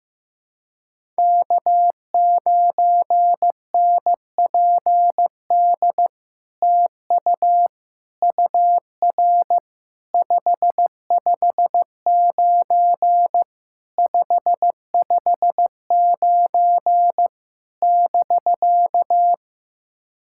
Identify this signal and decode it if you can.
Morse code, heard as a keyed tone, about 15 wpm, 700 Hz. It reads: K9NPD TU UR 559 559 <BK>
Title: SOTA QSO Practice 15wpm